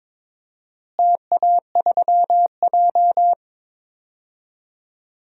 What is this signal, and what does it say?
Morse code, heard as a keyed tone, about 22 wpm, 700 Hz. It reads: TA3J
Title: DXCC-100 Call Signs - 4 Characters - Repeated 3x - 22wpm